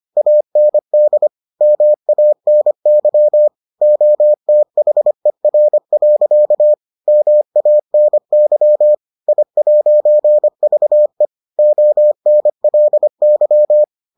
Morse code, heard as a keyed tone, about 25 wpm, 600 Hz. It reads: AND MANY OTHER. MANY I'VE ONLY